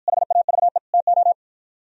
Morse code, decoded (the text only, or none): HAVE TO